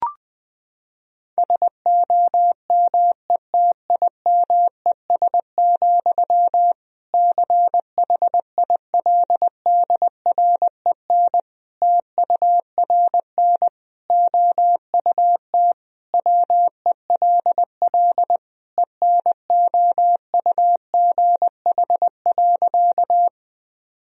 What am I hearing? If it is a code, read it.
Morse code, heard as a keyed tone, about 21 wpm, 700 Hz. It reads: SOMETIMES, CHILDREN TURN OUT WELL ENOUGH.
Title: Sentences from Top 1000 Words - Review Part 3 - 20wpm